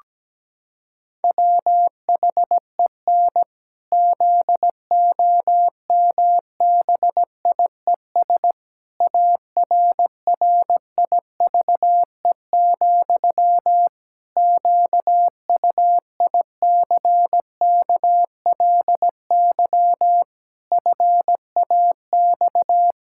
Morse code, heard as a keyed tone, about 17 wpm, 700 Hz. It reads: WHEN ZOMBIES ARRIVE, QUICKLY FAX
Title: Pangram Challenge v2 - 17wpm